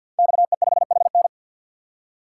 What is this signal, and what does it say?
Morse code, heard as a keyed tone, about 51 wpm, 700 Hz. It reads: XE5HN
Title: Code Groups - 5 Letter:Numbers - 50wpm